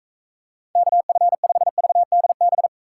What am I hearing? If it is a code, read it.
Morse code, heard as a keyed tone, about 44 wpm, 700 Hz. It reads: KF5VDB